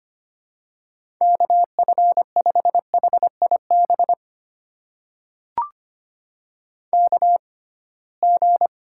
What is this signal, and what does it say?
Morse code, heard as a keyed tone, about 25 wpm, spaced as for 19 wpm, 700 Hz. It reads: KF5HIB KG
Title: USA Call Signs - 6 Characters - 6x Character Spacing - 25wpm